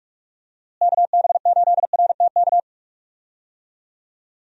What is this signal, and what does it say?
Morse code, heard as a keyed tone, about 48 wpm, 700 Hz. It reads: KB8RTK
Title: USA Call Signs - 6 Characters - Repeated 2x - 45wpm